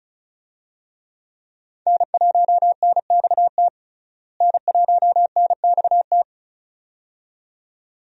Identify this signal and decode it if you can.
Morse code, heard as a keyed tone, about 36 wpm, 700 Hz. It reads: N1NXT N1NXT
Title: USA Call Signs - 5 Characters - Repeated 2x - 35wpm